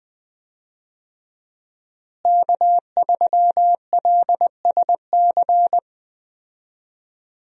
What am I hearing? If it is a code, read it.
Morse code, heard as a keyed tone, about 21 wpm, 700 Hz. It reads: K3LSC
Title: USA Call Signs - 5 Characters - Repeated 3x - 20wpm